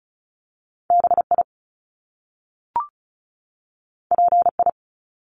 Morse code, heard as a keyed tone, about 36 wpm, 700 Hz. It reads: BI PI
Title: Top Two Letter Combinations in Words 35wpm